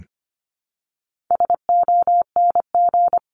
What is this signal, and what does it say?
Morse code, heard as a keyed tone, about 25 wpm, 700 Hz. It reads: SONG
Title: Top 500 Words Repeated 25wpm